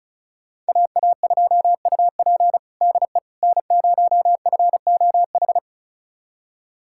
Morse code, heard as a keyed tone, about 36 wpm, 700 Hz. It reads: AA2UP DE N0FOH
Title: USA Callsign DE Callsign - 5 Characters - 35wpm